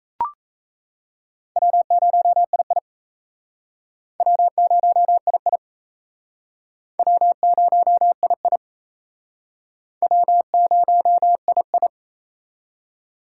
Morse code, heard as a keyed tone, about 34 wpm, 700 Hz. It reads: W0II W0II W0II W0II
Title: Speed Racing - USA 1x2 Call Signs 28wpm